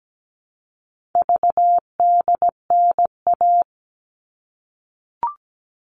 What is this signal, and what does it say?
Morse code, heard as a keyed tone, about 17 wpm, 700 Hz. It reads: VDNA
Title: All Cut Numbers - 4-Digit 17wpm